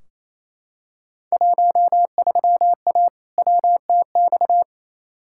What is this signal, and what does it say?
Morse code, heard as a keyed tone, about 28 wpm, 700 Hz. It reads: 13A WTX
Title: Field Day Exchanges Encore - 28wpm